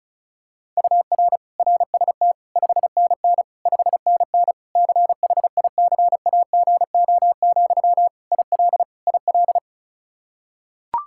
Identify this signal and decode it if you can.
Morse code, heard as a keyed tone, about 36 wpm, 700 Hz. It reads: UR RST 5NN 5NN CHICAGO, IL IL
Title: K1USN SST QSO Practice 35wpm